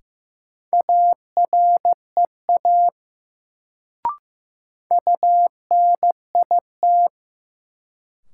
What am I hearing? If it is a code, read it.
Morse code, heard as a keyed tone, about 15 wpm, 700 Hz. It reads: AREA UNIT
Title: Sentences from Top 1000 Words - Review Part 2 - New Word Drill - 15wpm